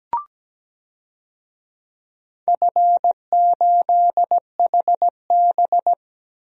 Morse code, heard as a keyed tone, about 17 wpm, 700 Hz. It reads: F8HB
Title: DXCC-100 Call Signs - 4 Characters - 17wpm